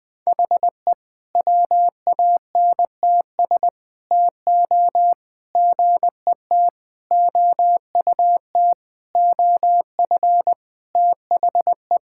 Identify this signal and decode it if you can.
Morse code, heard as a keyed tone, about 20 wpm, 700 Hz. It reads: HE WANTS TO GET OUT OF THE
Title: Sentences with 4-Word Phrases - 20wpm